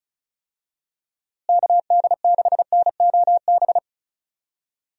Morse code, heard as a keyed tone, about 34 wpm, 700 Hz. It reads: KD6NOB
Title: USA Call Signs - 6 Characters - Repeated 3x - 35wpm